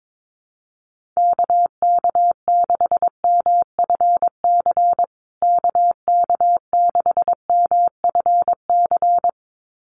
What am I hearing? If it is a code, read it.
Morse code, heard as a keyed tone, about 22 wpm, 700 Hz. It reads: KK6MFC KK6MFC
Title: USA Call Signs Repeated 2x 22wpm